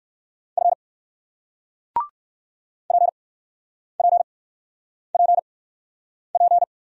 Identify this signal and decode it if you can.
Morse code, heard as a keyed tone, about 53 wpm, spaced as for 9 wpm, 700 Hz. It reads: F PPPP
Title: Speed-Racing Single Letter-Number 45wpm